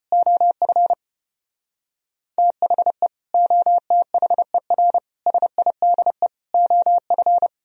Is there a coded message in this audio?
Yes: Morse code from 0.1 to 7.5 s, heard as a keyed tone, about 33 wpm, 700 Hz. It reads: OF THE OTHER SIDE OF